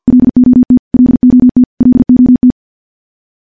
Morse code, heard as a keyed tone, about 50 wpm, 250 Hz. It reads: LOT LOT LOT
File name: morse / morse-training-50WPM